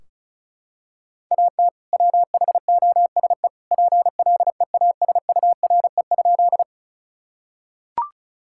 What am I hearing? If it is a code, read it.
Morse code, heard as a keyed tone, about 34 wpm, 700 Hz. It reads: AT WHOSE PLEASURE?
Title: Sentences from Top 1600 Words 35wpm